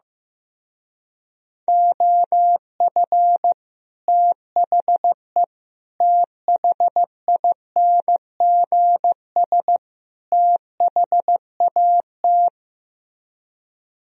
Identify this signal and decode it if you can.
Morse code, heard as a keyed tone, about 15 wpm, 700 Hz. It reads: OF THE THINGS THAT
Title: Speed-Racing - Sets of 4 Words 15wpm